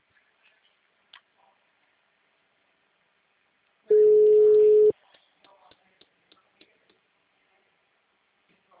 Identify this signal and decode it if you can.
Ringback tone (Europe, 425 Hz), with speech.
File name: out- -2011- - - .614